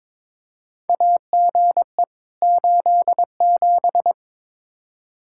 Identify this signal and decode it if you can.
Morse code, heard as a keyed tone, about 22 wpm, 700 Hz. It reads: AGE 87
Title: K1USN SST QSO Practice 22wpm